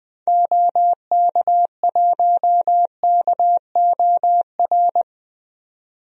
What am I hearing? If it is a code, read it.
Morse code, heard as a keyed tone, about 20 wpm, 700 Hz. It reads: OK1KOR
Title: DXCC-100 Call Signs - 6 Characters - 20wpm